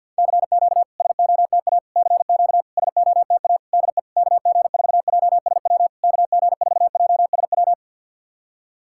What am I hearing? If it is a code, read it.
Morse code, heard as a keyed tone, about 53 wpm, 700 Hz. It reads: CQ SOTA CQ SOTA DE KG4JSW KG4JSW
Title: SOTA QSO Practice 50wpm